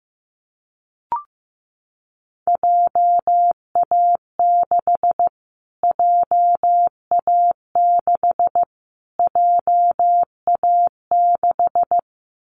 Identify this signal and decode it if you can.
Morse code, heard as a keyed tone, about 15 wpm, 700 Hz. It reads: JA6 JA6 JA6
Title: Callsign Prefixes for Top 25 DXCC - Repeated 3x - 15wpm